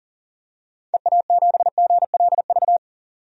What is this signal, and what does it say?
Morse code, heard as a keyed tone, about 38 wpm, 700 Hz. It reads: EA7GLV
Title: DXCC-100 Call Signs - 6 Characters - 40wpm